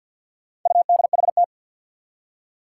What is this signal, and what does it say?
Morse code, heard as a keyed tone, about 52 wpm, 700 Hz. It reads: UDHT
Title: Four Challenging Letter-Number Combinations 50wpm